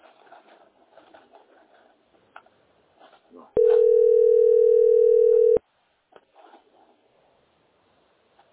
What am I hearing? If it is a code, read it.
Ringback tone (North America), 440 + 480 Hz.